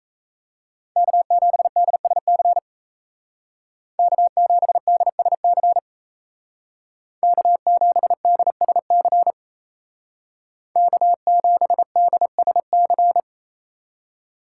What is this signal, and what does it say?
Morse code, heard as a keyed tone, about 33 wpm, 700 Hz. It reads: K7DSC K7DSC K7DSC K7DSC
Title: Speed Racing - USA Call Signs 28wpm